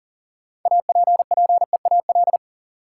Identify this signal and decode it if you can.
Morse code, heard as a keyed tone, about 39 wpm, 700 Hz. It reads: APPEAL
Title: Sentences from Top 1300 Words - New Word Drill - 40wpm